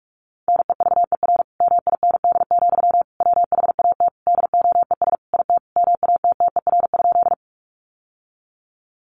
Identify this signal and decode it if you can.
Morse code, heard as a keyed tone, about 44 wpm, 700 Hz. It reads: NEVER MIND, WHAT DOES IT MATTER?
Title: Sentences from Top 700 Words 45wpm